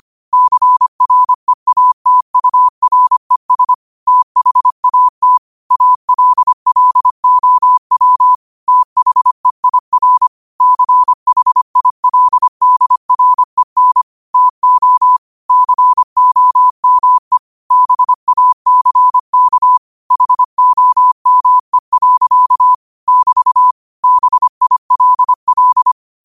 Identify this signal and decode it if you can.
Morse code, heard as a keyed tone, about 25 wpm, 1000 Hz. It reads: CREATURES THAT ALLOW THEIR CHILDREN TO COME BACK HOME. <BT> BILL